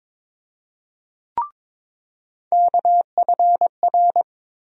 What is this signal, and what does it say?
Morse code, heard as a keyed tone, about 22 wpm, 700 Hz. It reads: KFR
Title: USA Call Sign Postfixes - 3 Letters 22wpm